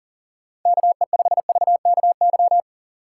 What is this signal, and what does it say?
Morse code, heard as a keyed tone, about 52 wpm, 700 Hz. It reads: KE5VKY